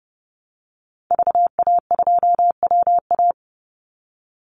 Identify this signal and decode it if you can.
Morse code, heard as a keyed tone, about 30 wpm, 700 Hz. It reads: VA2WA
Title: Call Signs Repeated 3x - 30wpm